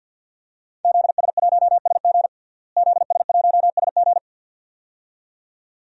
Morse code, heard as a keyed tone, about 47 wpm, 700 Hz. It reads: ZS1SG ZS1SG
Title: DXCC-100 Call Signs - 5 Characters - Repeated 2x - 50wpm